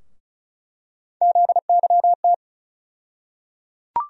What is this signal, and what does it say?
Morse code, heard as a keyed tone, about 34 wpm, 700 Hz. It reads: ZYT
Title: USA Call Sign Postfixes - 3 Letters 35wpm